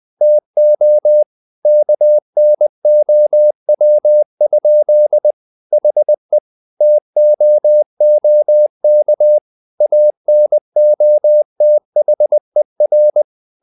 Morse code, heard as a keyed tone, about 20 wpm, 600 Hz. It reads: TO KNOW? HE TOOK ANOTHER
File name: the old man and the sea / the old man and the sea part 1 20wpm